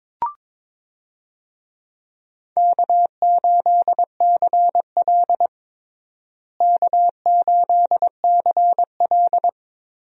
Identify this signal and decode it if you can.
Morse code, heard as a keyed tone, about 22 wpm, 700 Hz. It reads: K8CL K8CL
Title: USA Call Signs - 4 Characters - Repeated 2x - 22wpm